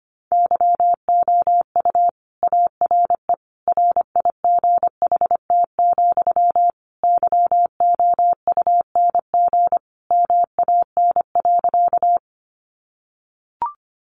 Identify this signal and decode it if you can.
Morse code, heard as a keyed tone, about 25 wpm, 700 Hz. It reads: YOU ARE RIGHT, YOUNG MAN.
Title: Sentences from Top 400 Words 25wpm